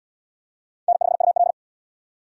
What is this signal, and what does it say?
Morse code, heard as a keyed tone, about 75 wpm, 700 Hz. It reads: N5UL